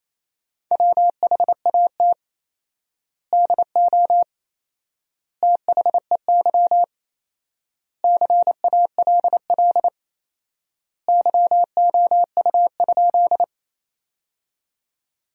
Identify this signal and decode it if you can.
Morse code, heard as a keyed tone, about 28 wpm, 700 Hz. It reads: WHAT DO THEY CALL YOU?
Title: 4x Word-Spacing - Sentences from Top 100 Words - 28wpm